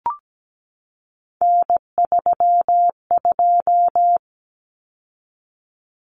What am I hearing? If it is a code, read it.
Morse code, heard as a keyed tone, about 17 wpm, 700 Hz. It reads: N32
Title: Common Cut Numbers - 3-Digit 17wpm